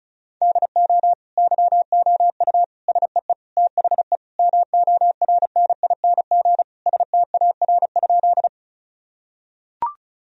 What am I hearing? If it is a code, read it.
Morse code, heard as a keyed tone, about 34 wpm, 700 Hz. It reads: DO YOU SEE THE MORNING STAR?
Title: Sentences from Top 1000 Words - Review Part 2 - 35wpm